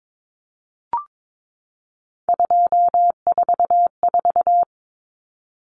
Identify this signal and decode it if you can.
Morse code, heard as a keyed tone, about 22 wpm, 700 Hz. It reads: 244
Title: Numbers - 3-Digit 22wpm